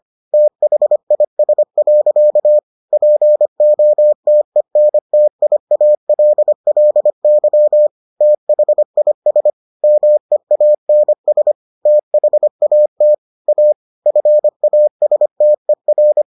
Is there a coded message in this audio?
Yes: Morse code from 0.3 to 16.2 s, heard as a keyed tone, about 25 wpm, 600 Hz. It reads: THIS. POTENTIALLY THIS MEANS THAT A FASTER